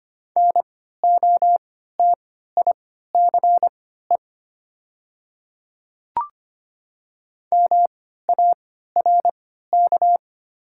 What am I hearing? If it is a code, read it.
Morse code, heard as a keyed tone, about 25 wpm, spaced as for 14 wpm, 700 Hz. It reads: NOTICE MARK
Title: 3x Character Spacing - Top 500 Words - 25wpm